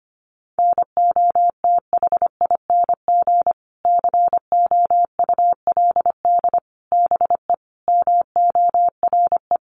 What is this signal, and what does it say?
Morse code, heard as a keyed tone, about 25 wpm, 700 Hz. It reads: NOTHING COULD BE MORE
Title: Sentences from Top 500 Words 25wpm